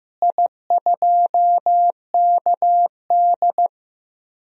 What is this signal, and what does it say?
Morse code, heard as a keyed tone, about 15 wpm, 700 Hz. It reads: I2KD
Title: DXCC-100 Call Signs - 4 Characters - Repeated 2x - 15wpm